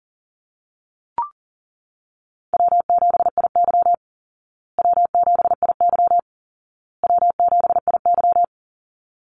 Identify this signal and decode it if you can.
Morse code, heard as a keyed tone, about 40 wpm, 700 Hz. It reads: W7IY W7IY W7IY
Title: Easy Call Signs Repeated 3x - 40wpm